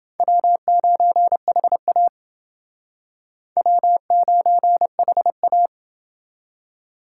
Morse code, heard as a keyed tone, about 29 wpm, 700 Hz. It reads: W9HA W9HA